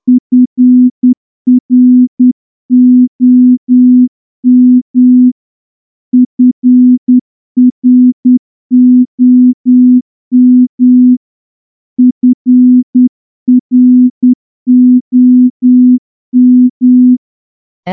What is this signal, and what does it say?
Morse code, heard as a keyed tone, about 10 wpm, 250 Hz. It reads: FROM FROM FROM
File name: morse-training-10WPM